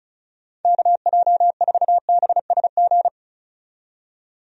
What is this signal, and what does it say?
Morse code, heard as a keyed tone, about 35 wpm, 700 Hz. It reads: KJ4BSG
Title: USA Call Signs - 6 Characters - 35wpm